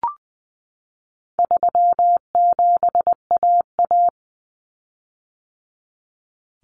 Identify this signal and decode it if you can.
Morse code, heard as a keyed tone, about 20 wpm, 700 Hz. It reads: 37AA
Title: Common Cut Numbers - 4-Digit 20wpm